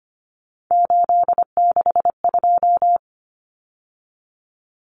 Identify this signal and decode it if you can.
Morse code, heard as a keyed tone, about 25 wpm, 700 Hz. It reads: 862
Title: Common Cut Numbers - 3-Digit 25wpm